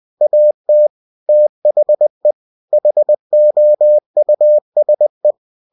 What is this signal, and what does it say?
Morse code, heard as a keyed tone, about 20 wpm, 600 Hz. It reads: AT THE HOUSE
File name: cw-books / winnie-the-pooh / winnie-20wpm-page-0005